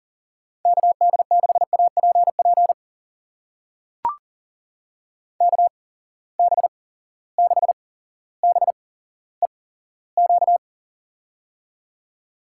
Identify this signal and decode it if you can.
Morse code, heard as a keyed tone, about 40 wpm, spaced as for 17 wpm, 700 Hz. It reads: KD6APP KB6BEQ
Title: USA Call Signs - 6 Characters - 8x Character Spacing - 40wpm